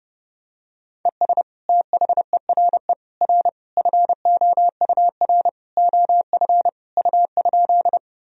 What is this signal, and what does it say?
Morse code, heard as a keyed tone, about 32 wpm, 700 Hz. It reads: ES THERE R FOUR OF U?
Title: Sentences from Top 300 Words and Abbreviations 30wpm